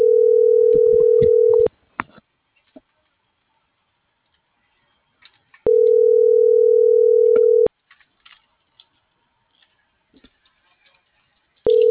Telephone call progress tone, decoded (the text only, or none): ringback tone